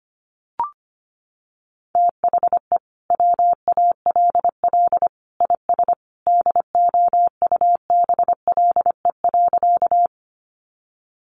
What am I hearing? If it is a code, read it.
Morse code, heard as a keyed tone, about 25 wpm, 700 Hz. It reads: THE WALL IS DOUBLE.
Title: Sentences from Top 1400 Words 25wpm